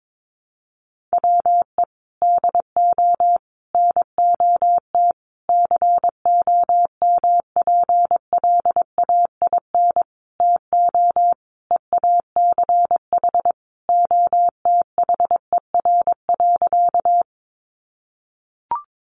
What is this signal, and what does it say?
Morse code, heard as a keyed tone, about 22 wpm, 700 Hz. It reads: WE DO NOT COMPLAIN TO EACH OTHER.
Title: Sentences from Top 1300 Words 22wpm